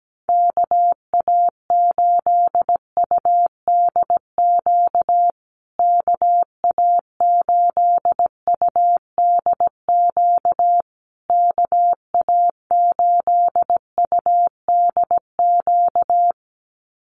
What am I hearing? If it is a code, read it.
Morse code, heard as a keyed tone, about 17 wpm, 700 Hz. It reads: KA8UDQ KA8UDQ KA8UDQ
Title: USA Call Signs Repeated 3x 17wpm